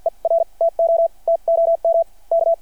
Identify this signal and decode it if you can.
Morse code, heard as a keyed tone, about 53 wpm, spaced as for 41 wpm, 690 Hz. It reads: ER TO TOM K
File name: archived / CRSnet